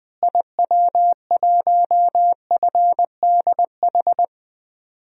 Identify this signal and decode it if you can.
Morse code, heard as a keyed tone, about 20 wpm, 700 Hz. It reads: IW1FDH
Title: DXCC-100 Call Signs - 6 Characters - Repeated 3x - 20wpm